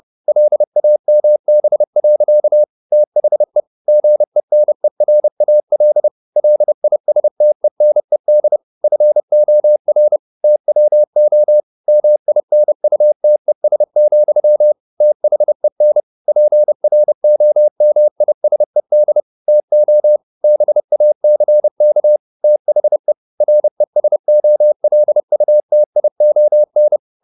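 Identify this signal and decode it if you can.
Morse code, heard as a keyed tone, about 30 wpm, 600 Hz. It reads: LAMB. THE GENERAL LISTENED FOR TWO MINUTES, THEN PROMISED TO BACK THE RESOLUTION
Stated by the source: Morse code at 30 wpm, the novel Never by Ken Follett in English, from CW med Gnister.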